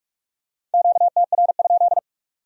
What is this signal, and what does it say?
Morse code, heard as a keyed tone, about 39 wpm, 700 Hz. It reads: QTR?